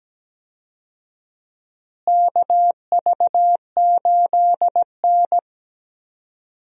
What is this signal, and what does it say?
Morse code, heard as a keyed tone, about 17 wpm, 700 Hz. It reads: KV8N
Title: USA Call Signs - 4 Characters - Repeated 3x - 17wpm